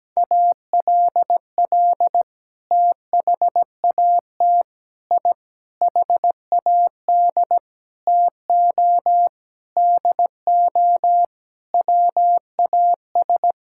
Morse code, heard as a keyed tone, about 17 wpm, 700 Hz. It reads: ALL THAT I HAD TO DO WAS
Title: Sentences from Top 1000 Words - Review Part 6 - 17wpm